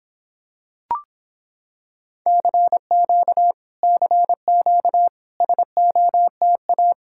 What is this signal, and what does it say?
Morse code, heard as a keyed tone, about 26 wpm, 700 Hz. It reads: CQ CQ SOTA